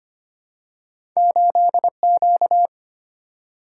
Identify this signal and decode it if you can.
Morse code, heard as a keyed tone, about 25 wpm, 700 Hz. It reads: 8Q
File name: Two Letter-Number Combinations - 25wpm